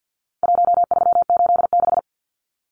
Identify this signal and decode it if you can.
Morse code, heard as a keyed tone, about 50 wpm, 700 Hz. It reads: 1386
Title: Numbers - Contest Style - 4-Digit 50wpm